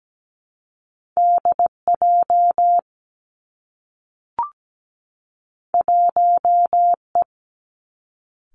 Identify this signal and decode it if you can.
Morse code, heard as a keyed tone, about 17 wpm, 700 Hz. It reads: DJ 1E
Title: Two Challenging Letter-Number Combinations 17wpm